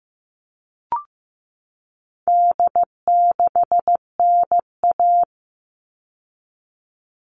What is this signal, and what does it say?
Morse code, heard as a keyed tone, about 15 wpm, 700 Hz. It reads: D6NA
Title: All Cut Numbers - 4-Digit 15wpm